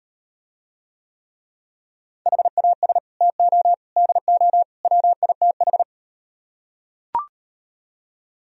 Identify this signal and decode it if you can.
Morse code, heard as a keyed tone, about 38 wpm, 700 Hz. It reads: HAS TO DO WITH